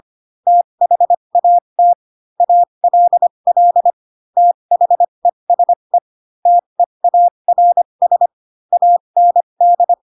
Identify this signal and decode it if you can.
Morse code, heard as a keyed tone, about 25 wpm, spaced as for 22 wpm, 700 Hz. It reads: THAT ALL THESE TEARS AND
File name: Pinocchio / Pinocchio ch7 18 25 ik1hge